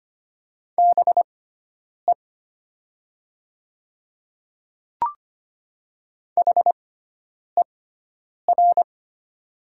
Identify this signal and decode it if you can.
Morse code, heard as a keyed tone, about 25 wpm, spaced as for 9 wpm, 700 Hz. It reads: BE HER
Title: 6x Character Spacing - Top 100 Words - 25wpm